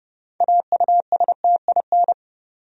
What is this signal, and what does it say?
Morse code, heard as a keyed tone, about 30 wpm, 700 Hz. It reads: AUSTIN
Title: USA Common Names - 30wpm